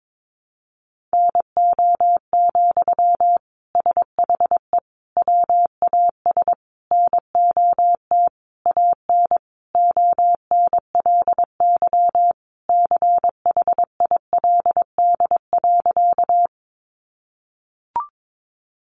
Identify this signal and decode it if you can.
Morse code, heard as a keyed tone, about 22 wpm, 700 Hz. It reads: NO, SHE WAS NOT AN ONLY CHILD.
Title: Sentences from Top 800 Words 22wpm